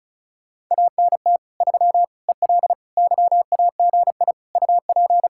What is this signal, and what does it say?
Morse code, heard as a keyed tone, about 45 wpm, 700 Hz. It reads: ANT 3 EL YAGI UP